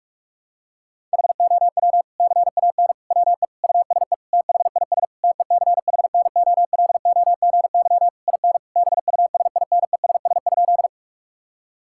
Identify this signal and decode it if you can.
Morse code, heard as a keyed tone, about 46 wpm, 700 Hz. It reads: HOW CAN WE USE THIS TECHNOLOGY IN BUSINESS?
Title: Sentences from Top 1000 Words - Review Part 5 - 45wpm